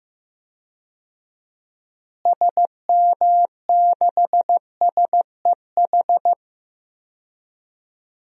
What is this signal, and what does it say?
Morse code, heard as a keyed tone, about 15 wpm, 700 Hz. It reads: SM6SEH
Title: DXCC-100 Call Signs - 6 Characters - Repeated 2x - 15wpm